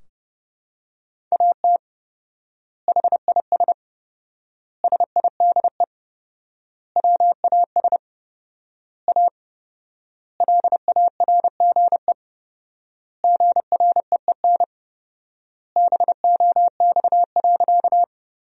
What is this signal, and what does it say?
Morse code, heard as a keyed tone, about 30 wpm, 700 Hz. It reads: AT HIS SIDE WAS A LARGE GREEN BOX.
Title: 4x Word-Spacing - Sentences from Top 500 Words - 30wpm